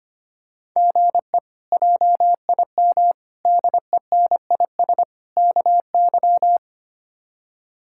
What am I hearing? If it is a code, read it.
Morse code, heard as a keyed tone, about 25 wpm, 700 Hz. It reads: GE JIM DENIS KY
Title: K1USN SST QSO Practice - 25wpm